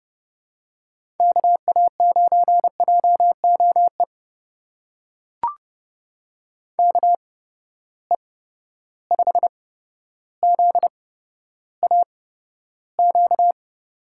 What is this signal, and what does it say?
Morse code, heard as a keyed tone, about 31 wpm, spaced as for 13 wpm, 700 Hz. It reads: KA9JOE KE5ZAQ
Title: USA Call Signs - 6 Characters - 8x Character Spacing - 30wpm